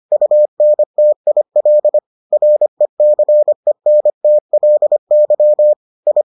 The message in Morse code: UNTIL RECENTLY I

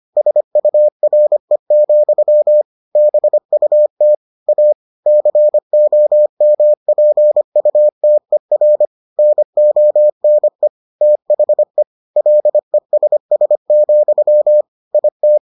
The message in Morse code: SURE, BUT A COMPUTER NONE THE LESS, IT

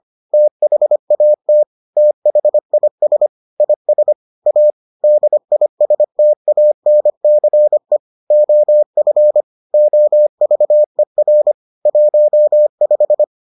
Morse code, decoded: THAT THIS IS A DISTANCE OF OVER 15

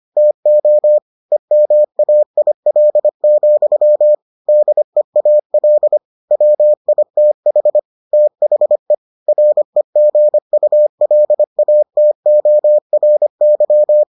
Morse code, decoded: TO EMAIL, DEAL WITH THE REGULATORY

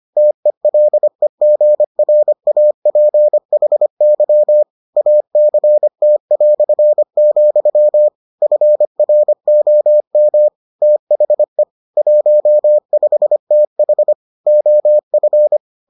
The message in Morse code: TELEGRAPHY ACT", FROM THE 15TH OF

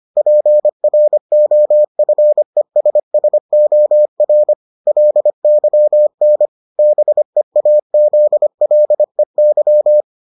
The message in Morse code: PROFESSOR LYN BEAZLEY

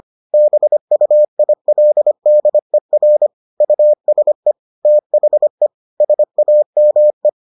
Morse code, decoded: BUILDER USE THE SAME

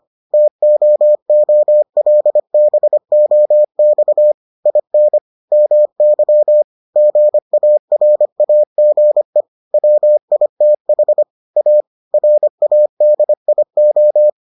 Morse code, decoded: TOOLBOX IN MY GARAGE WITH A RADIO

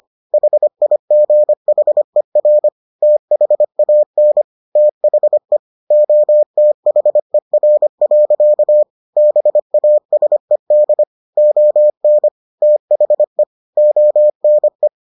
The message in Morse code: HIGHER THAN THE OTHER. BASED ON THE ONE

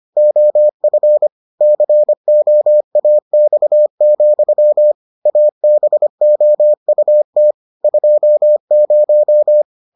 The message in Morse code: OF COAX, ABOUT 20

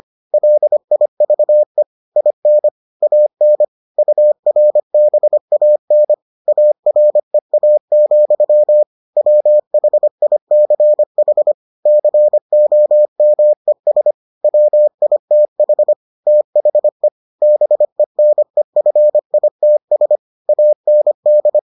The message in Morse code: LIVE IN AN URBAN AREA, WHICH COMES WITH THE BENEFITS AND

